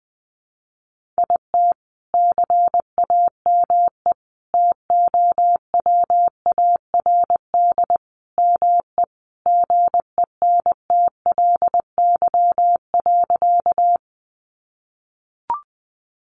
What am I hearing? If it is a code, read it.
Morse code, heard as a keyed tone, about 19 wpm, 700 Hz. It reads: IT CAME TOWARD ME GENTLY.
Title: Sentences from Top 1500 Words 20wpm